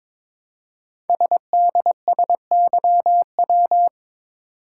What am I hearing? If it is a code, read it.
Morse code, heard as a keyed tone, about 22 wpm, 700 Hz. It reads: SDSYW
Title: Code Groups - 5 Letter:Numbers - 22wpm